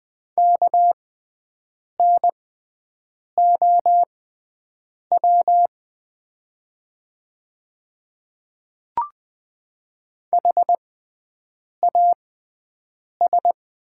Morse code, heard as a keyed tone, about 20 wpm, spaced as for 7 wpm, 700 Hz. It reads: KNOW HAS
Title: 6x Character Spacing - Top 200 Words - 20wpm